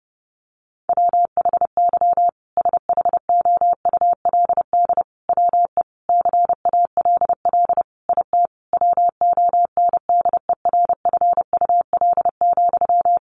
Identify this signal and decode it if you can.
Morse code, heard as a keyed tone, about 30 wpm, 700 Hz. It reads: WHY SHOULD WE CALL IT WONDERFUL,
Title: Sentences from Top 1100 Words 30wpm